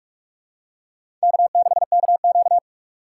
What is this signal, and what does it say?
Morse code, heard as a keyed tone, about 47 wpm, 690 Hz. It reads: K6KQ
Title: USA Call Signs - 4 Characters - 4x Character Spacing - 45wpm